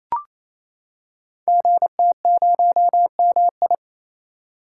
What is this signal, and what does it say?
Morse code, heard as a keyed tone, about 28 wpm, 700 Hz. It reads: GT0MI
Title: Code Groups - 5 Letter:Numbers - 28wpm